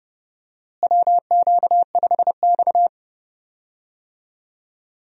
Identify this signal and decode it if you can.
Morse code, heard as a keyed tone, about 31 wpm, 700 Hz. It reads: WQ5X
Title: USA Call Signs - 4 Characters - Repeated 2x - 30wpm